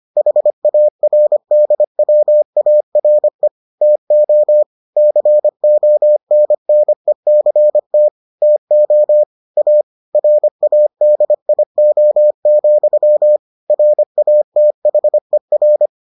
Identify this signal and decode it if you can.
Morse code, heard as a keyed tone, about 25 wpm, 600 Hz. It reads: HARDWARE TO CONNECT TO A RADIO, RATHER